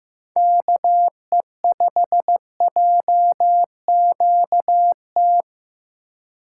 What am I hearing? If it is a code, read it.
Morse code, heard as a keyed tone, about 15 wpm, 700 Hz. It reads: KE5JQT
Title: USA Call Signs - 4x Character Spacing - 15wpm